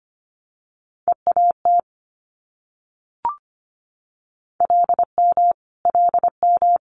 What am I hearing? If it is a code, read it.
Morse code, heard as a keyed tone, about 25 wpm, 700 Hz. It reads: EAT LM LM
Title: Multi-letter Phonemes Repeated 3x 25wpm